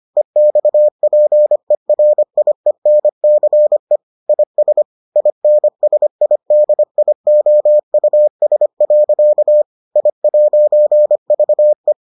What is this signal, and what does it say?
Morse code, heard as a keyed tone, about 25 wpm, 600 Hz. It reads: EXPERIENCE IS INSIDIOUS. I'VE